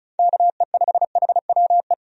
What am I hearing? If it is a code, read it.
Morse code, heard as a keyed tone, about 34 wpm, 700 Hz. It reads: KE5HWE
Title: USA Call Signs - 6 Characters - 35wpm